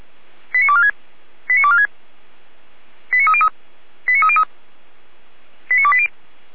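Five-tone selective call (ZVEI), heard as quick runs of five tones, tones of about 70 ms. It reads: 89246 89246 89303 89303 89280
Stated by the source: ZVEI five-tone selective call 89187, received by SDR.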